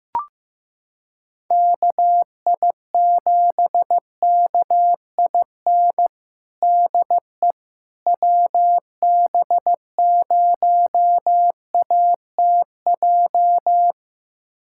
Morse code, heard as a keyed tone, about 15 wpm, 700 Hz. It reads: KI7KIN DE WB0ATJ
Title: USA Callsign DE Callsign 15wpm